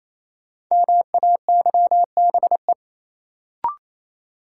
Morse code, heard as a keyed tone, about 28 wpm, 700 Hz. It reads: MAYBE
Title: Sentences from Top 1000 Words - New Word Drill - 28wpm